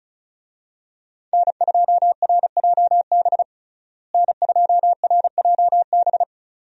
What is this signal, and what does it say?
Morse code, heard as a keyed tone, about 36 wpm, 700 Hz. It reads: N2RJB N2RJB
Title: USA Call Signs - 5 Characters - Repeated 2x - 35wpm